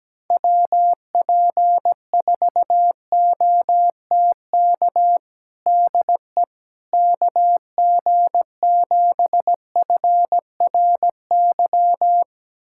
Morse code, heard as a keyed tone, about 17 wpm, 700 Hz. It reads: WP4OTK DE KG7FRY
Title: USA Callsign DE Callsign - 6 Characters - 17wpm